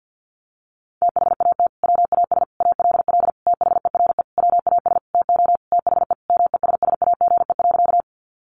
Morse code, heard as a keyed tone, about 50 wpm, 700 Hz. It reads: THAT WAS ALL THERE WAS TO THE MESSAGE.
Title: Sentences from Top 1100 Words 50wpm